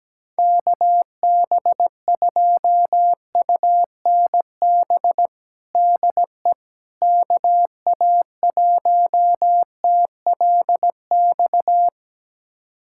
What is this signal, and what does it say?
Morse code, heard as a keyed tone, about 17 wpm, 700 Hz. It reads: KB2UNB DE KA1TLX
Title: USA Callsign DE Callsign - 6 Characters - 17wpm